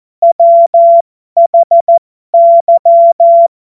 Morse code, heard as a keyed tone, about 14 wpm, spaced as for 12 wpm, 680 Hz. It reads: WHY
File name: Morsekurs / Witze